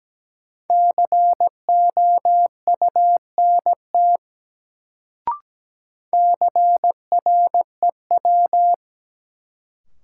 Morse code, heard as a keyed tone, about 17 wpm, 700 Hz. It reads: COUNT CREW
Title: Sentences from Top 1500 Words - New Word Drill Alt - 17wpm